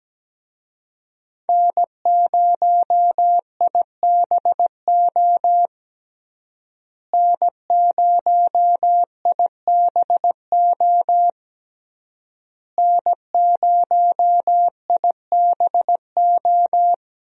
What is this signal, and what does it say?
Morse code, heard as a keyed tone, about 17 wpm, 700 Hz. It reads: N0IBO N0IBO N0IBO
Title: USA Call Signs - 5 Characters - Repeated 3x - 17wpm